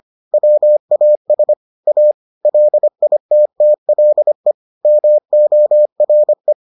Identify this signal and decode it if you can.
Morse code, heard as a keyed tone, about 25 wpm, 600 Hz. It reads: WAS A LITTLE MORE